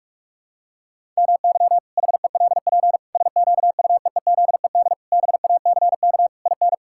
Morse code, heard as a keyed tone, about 41 wpm, 700 Hz. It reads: MY HELP SQUEEZED BACK IN
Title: Pangram Challenge v2 - 45wpm